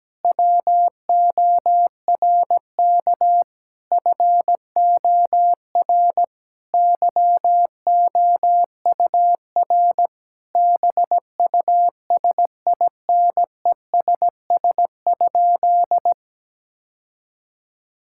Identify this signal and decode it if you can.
Morse code, heard as a keyed tone, about 17 wpm, 700 Hz. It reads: WORK FOR YOUR BUSINESS?
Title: Sentences from Top 1000 Words - Review Part 6 - 17wpm